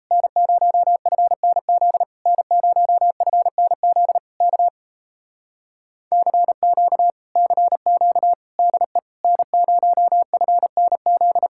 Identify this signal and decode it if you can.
Morse code, heard as a keyed tone, about 37 wpm, 700 Hz. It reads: N0FNZ N0FNZ K CQ CQ DE N0FNZ